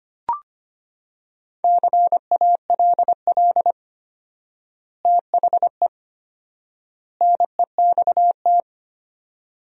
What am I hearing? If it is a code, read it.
Morse code, heard as a keyed tone, about 25 wpm, 700 Hz. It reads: CALL THE NEXT
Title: 4x Word-Spacing - Sentences from Top 300 Words - 25wpm